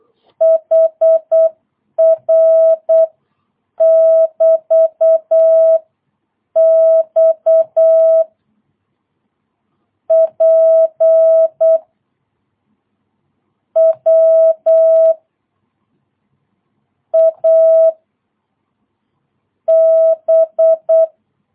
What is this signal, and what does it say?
Morse code, heard as a keyed tone, about 8 wpm, spaced as for 7 wpm, 660 Hz. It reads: HR<BT>X P W A B
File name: jh8cbh / 2022cw